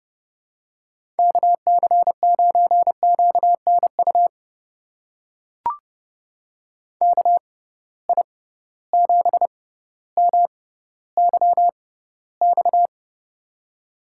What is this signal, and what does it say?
Morse code, heard as a keyed tone, about 30 wpm, spaced as for 15 wpm, 700 Hz. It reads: KC9QNU KI7MYX
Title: USA Call Signs - 6 Characters - 6x Character Spacing - 30wpm